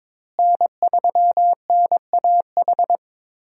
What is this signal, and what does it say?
Morse code, heard as a keyed tone, about 22 wpm, 700 Hz. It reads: N3NAH
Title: USA Call Signs - 5 Characters - Repeated 2x - 22wpm